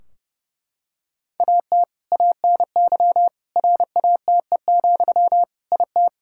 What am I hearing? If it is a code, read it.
Morse code, heard as a keyed tone, about 30 wpm, 700 Hz. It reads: AT ANY RATE, IT
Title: Sentences from Top 1600 Words 30wpm